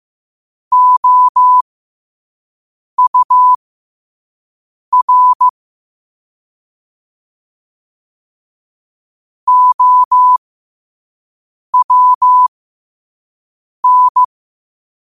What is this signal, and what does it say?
Morse code, heard as a keyed tone, about 15 wpm, spaced as for 5 wpm, 1000 Hz. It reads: OUR OWN
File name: qotdfiles / 5wpm